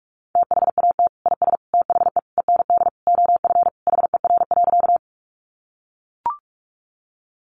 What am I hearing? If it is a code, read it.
Morse code, heard as a keyed tone, about 44 wpm, 700 Hz. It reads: THAT IS THE END OF HER.
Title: Sentences from Top 200 Words 45wpm